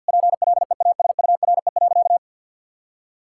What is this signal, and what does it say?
Morse code, heard as a keyed tone, about 52 wpm, 700 Hz. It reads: PLEASURE.
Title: Sentences from Top 1600 Words 50wpm